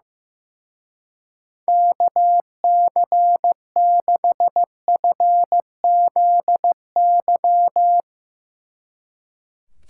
Morse code, heard as a keyed tone, about 16 wpm, 700 Hz. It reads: KC6FZY
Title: Speed Racing - USA Call Signs 15wpm